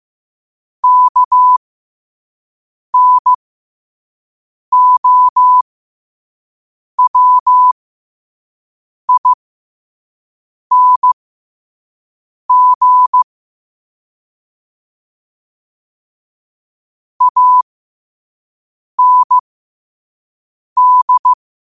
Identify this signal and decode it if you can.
Morse code, heard as a keyed tone, about 15 wpm, spaced as for 5 wpm, 1000 Hz. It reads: KNOWING AND
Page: qotd-at-5-wpm